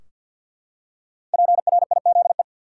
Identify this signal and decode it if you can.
Morse code, heard as a keyed tone, about 45 wpm, 700 Hz. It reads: PRIZE
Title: Five Letter Words - 50wpm